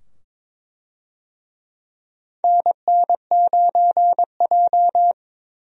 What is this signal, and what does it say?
Morse code, heard as a keyed tone, about 22 wpm, 700 Hz. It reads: NN9J
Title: USA Call Signs - 4 Characters - 22wpm